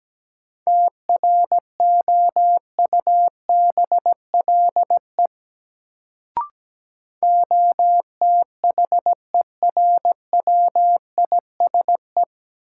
Morse code, heard as a keyed tone, about 17 wpm, 700 Hz. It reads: TROUBLE OTHERWISE
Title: Sentences from Top 1200 Words - New Word Drill - 17wpm